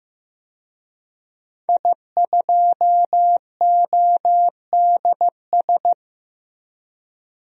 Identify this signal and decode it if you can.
Morse code, heard as a keyed tone, about 15 wpm, 700 Hz. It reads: I2ODS
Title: DXCC-100 Call Signs - 5 Characters - 15wpm